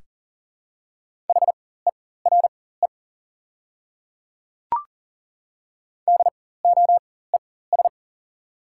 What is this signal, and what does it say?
Morse code, heard as a keyed tone, about 40 wpm, spaced as for 19 wpm, 690 Hz. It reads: HERE DOES
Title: 4x Character Spacing - Top 200 Words - 40wpm